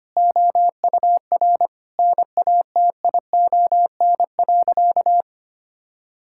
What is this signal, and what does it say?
Morse code, heard as a keyed tone, about 25 wpm, 700 Hz. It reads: OUR NATION.
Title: Sentences from Top 1000 Words - Review Part 6 - 25wpm